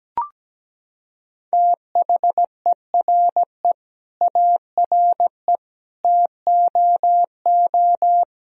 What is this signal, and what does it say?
Morse code, heard as a keyed tone, about 17 wpm, 700 Hz. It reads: THERE ARE TOO
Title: Sentences from Top 1000 Words - Review Part 3 - 17wpm